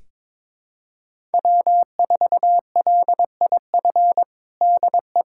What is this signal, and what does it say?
Morse code, heard as a keyed tone, about 22 wpm, 700 Hz. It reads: W4LIF DE
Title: USA Callsign DE Callsign - 5 Characters - 22wpm